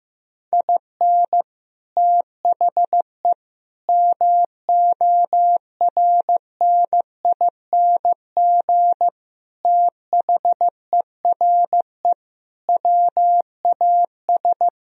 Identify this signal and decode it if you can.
Morse code, heard as a keyed tone, about 15 wpm, 700 Hz. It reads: IN THE MORNING THERE WAS
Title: Sentences from Top 1000 Words - Review Part 1 - 15wpm